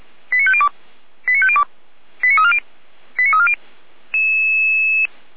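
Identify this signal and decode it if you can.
Five-tone selective call (ZVEI), heard as quick runs of five tones, tones of about 70 ms. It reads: 89502 89502 89340 89340
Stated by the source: ZVEI five-tone selective call received by SDR.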